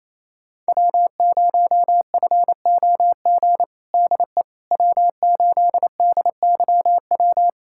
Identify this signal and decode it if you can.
Morse code, heard as a keyed tone, about 28 wpm, 700 Hz. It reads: W0FOG DE W8DYW
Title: USA Callsign DE Callsign - 5 Characters - 28wpm